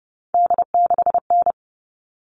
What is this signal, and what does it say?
Morse code, heard as a keyed tone, about 30 wpm, 700 Hz. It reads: D6N
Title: All Cut Numbers - 3-Digit 30wpm